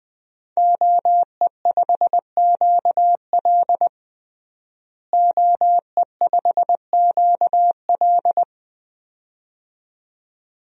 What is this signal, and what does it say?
Morse code, heard as a keyed tone, about 20 wpm, 700 Hz. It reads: OE5QL OE5QL
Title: DXCC-100 Call Signs - 5 Characters - Repeated 2x - 20wpm